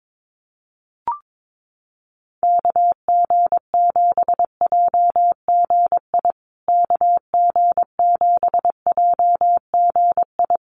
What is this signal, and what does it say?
Morse code, heard as a keyed tone, about 22 wpm, 700 Hz. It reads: KG7JGI KG7JGI
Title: USA Call Signs Repeated 3x 22wpm